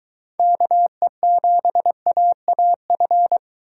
Morse code, heard as a keyed tone, about 22 wpm, 700 Hz. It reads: KE7AAF